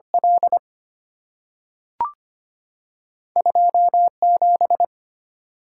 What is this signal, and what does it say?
Morse code, heard as a keyed tone, about 25 wpm, 700 Hz. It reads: L 27
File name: Two Letter-Number Combinations - 25wpm